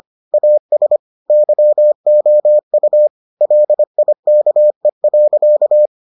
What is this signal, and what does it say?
Morse code, heard as a keyed tone, about 25 wpm, 600 Hz. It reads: AS YOU LIKE.